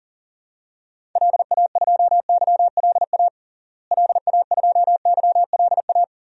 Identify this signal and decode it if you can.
Morse code, heard as a keyed tone, about 40 wpm, 700 Hz. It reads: LA2YLA LA2YLA
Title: DXCC-100 Call Signs - 6 Characters - Repeated 3x - 40wpm